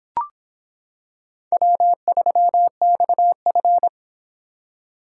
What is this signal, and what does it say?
Morse code, heard as a keyed tone, about 26 wpm, 700 Hz. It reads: W3XF